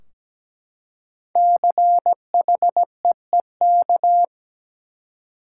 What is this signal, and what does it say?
Morse code, heard as a keyed tone, about 17 wpm, 700 Hz. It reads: CHEEK
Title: Five Letter Words - 17wpm